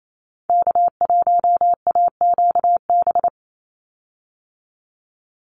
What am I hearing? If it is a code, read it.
Morse code, heard as a keyed tone, about 28 wpm, 700 Hz. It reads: K1AQB
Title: Call Signs Repeated 2x - 28wpm